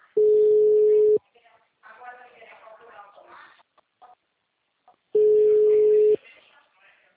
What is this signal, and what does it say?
Ringback tone (Europe, 425 Hz), with speech.